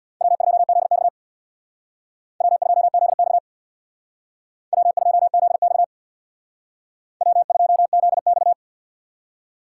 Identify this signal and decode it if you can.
Morse code, heard as a keyed tone, about 62 wpm, 700 Hz. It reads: W2ZX W2ZX W2ZX W2ZX
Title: Speed Racing - USA 1x2 Call Signs 50wpm